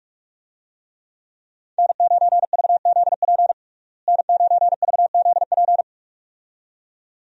Morse code, heard as a keyed tone, about 47 wpm, 700 Hz. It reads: N9VZP N9VZP
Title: USA Call Signs - 5 Characters - Repeated 2x - 45wpm